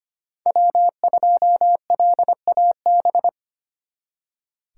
Morse code, heard as a keyed tone, about 32 wpm, 700 Hz. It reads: W2LAB